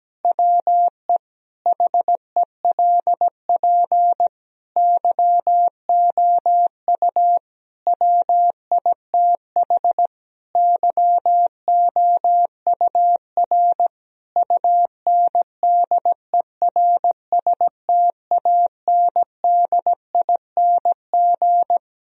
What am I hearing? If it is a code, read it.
Morse code, heard as a keyed tone, about 17 wpm, 700 Hz. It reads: WE HELP YOU WITH YOUR UNDERSTANDING
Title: Sentences from Top 1000 Words - Review Part 3 - 17wpm